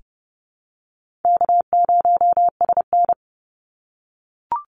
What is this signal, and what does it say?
Morse code, heard as a keyed tone, about 30 wpm, 700 Hz. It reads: K0SN
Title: Call Signs - 30wpm